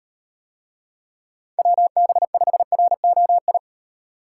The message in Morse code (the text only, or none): WB5ROI